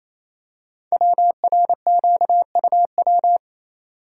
Morse code, heard as a keyed tone, about 29 wpm, 700 Hz. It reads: WRQUW
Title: Code Groups - 5 Letters - 28wpm